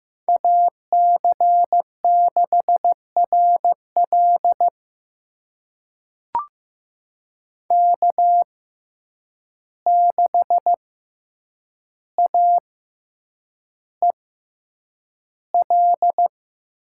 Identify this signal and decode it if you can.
Morse code, heard as a keyed tone, about 15 wpm, spaced as for 8 wpm, 700 Hz. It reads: AC6RL K6AEL
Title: USA Call Signs - 5 Characters - 6x Character Spacing - 15wpm